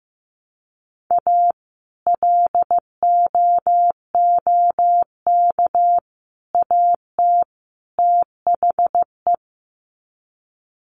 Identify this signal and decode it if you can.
Morse code, heard as a keyed tone, about 15 wpm, 700 Hz. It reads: A LOOK AT THE
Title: Sets of 4 Words 15wpm